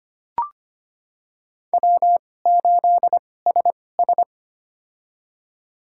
Morse code, heard as a keyed tone, about 25 wpm, spaced as for 18 wpm, 700 Hz. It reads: W8SS
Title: USA Call Signs - 4 Characters - 2x Character Spacing - 25wpm